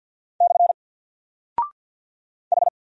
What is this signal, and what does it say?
Morse code, heard as a keyed tone, about 48 wpm, 700 Hz. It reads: / H
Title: Single Character (Letter:Number:Punctuation:Prosign) 50wpm